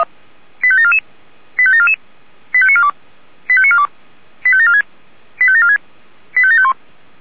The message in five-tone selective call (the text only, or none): 85744 85744 85942 85942 85746 85746 85751